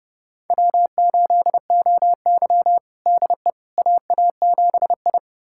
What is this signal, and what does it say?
Morse code, heard as a keyed tone, about 30 wpm, 700 Hz. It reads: W8OY DE AA7I
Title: USA Callsign DE Callsign - 4 Characters - 30wpm